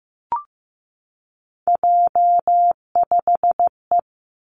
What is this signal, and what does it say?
Morse code, heard as a keyed tone, about 15 wpm, 700 Hz. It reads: J5E
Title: Three Challenging Letter-Number Combinations 15wpm